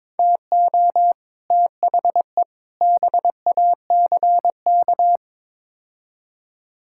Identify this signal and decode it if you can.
Morse code, heard as a keyed tone, about 21 wpm, 700 Hz. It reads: TO THE BACK
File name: practice / 22wpm / Build A Phrase - 5-Word Phrases - 22wpm